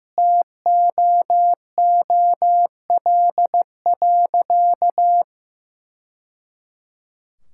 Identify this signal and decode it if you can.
Morse code, heard as a keyed tone, about 16 wpm, 700 Hz. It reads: TOOL.
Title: Sentences from Top 1000 Words - Review Part 6 - 15wpm